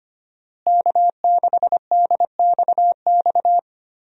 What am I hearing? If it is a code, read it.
Morse code, heard as a keyed tone, about 25 wpm, 700 Hz. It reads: K6DXX
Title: USA Call Signs - 5 Characters - Repeated 3x - 25wpm